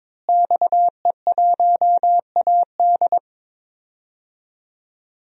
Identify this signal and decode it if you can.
Morse code, heard as a keyed tone, about 21 wpm, 700 Hz. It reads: XE1AD
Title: DXCC-100 Call Signs - 5 Characters - Repeated 2x - 22wpm